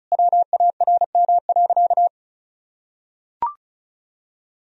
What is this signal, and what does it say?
Morse code, heard as a keyed tone, about 36 wpm, 700 Hz. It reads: WARM.
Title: Sentences from Top 1000 Words - Review Part 4 - 35wpm